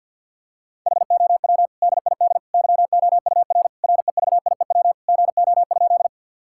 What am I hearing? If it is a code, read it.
Morse code, heard as a keyed tone, about 50 wpm, 700 Hz. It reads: HOW DID YOUR REVIEW GO?
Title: Sentences from Top 1000 Words - Review Part 6 - 50wpm